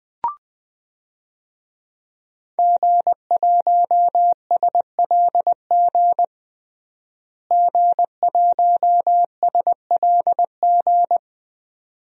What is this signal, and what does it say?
Morse code, heard as a keyed tone, about 20 wpm, 700 Hz. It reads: G1SLG G1SLG
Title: DXCC-100 Call Signs - 5 Characters - Repeated 2x - 20wpm